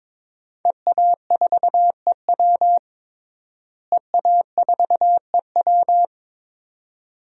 Morse code, heard as a keyed tone, about 21 wpm, 700 Hz. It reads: EA4EW EA4EW
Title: DXCC-100 Call Signs - 5 Characters - Repeated 2x - 22wpm